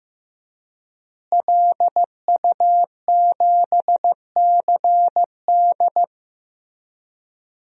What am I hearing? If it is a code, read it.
Morse code, heard as a keyed tone, about 15 wpm, 700 Hz. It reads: LU7CD
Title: DXCC-100 Call Signs - 5 Characters - Repeated 3x - 15wpm